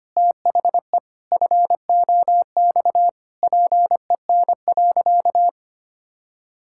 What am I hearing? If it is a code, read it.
Morse code, heard as a keyed tone, about 25 wpm, 700 Hz. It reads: THE FOX PEN.
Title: Pangram Challenge v2 - 25wpm